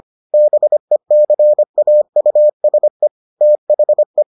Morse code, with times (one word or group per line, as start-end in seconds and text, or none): 0.3-3.1 BECAUSE
3.4-4.2 THE